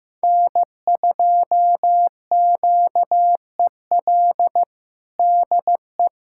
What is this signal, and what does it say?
Morse code, heard as a keyed tone, about 15 wpm, 700 Hz. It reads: N2QEL DE
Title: USA Callsign DE Callsign - 5 Characters - 15wpm